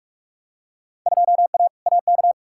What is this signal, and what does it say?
Morse code, heard as a keyed tone, about 42 wpm, 700 Hz. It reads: 2A AK
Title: Field Day Exchanges Encore - 45wpm